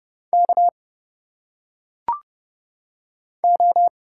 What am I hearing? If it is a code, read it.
Morse code, heard as a keyed tone, about 30 wpm, 700 Hz. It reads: K O
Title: Single Letters - Phonetic Pronunciation 30wpm